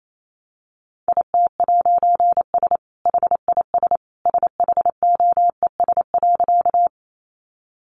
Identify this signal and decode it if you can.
Morse code, heard as a keyed tone, about 28 wpm, 700 Hz. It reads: IT'S HIS SHOES.
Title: A Sound Of Thunder 28wpm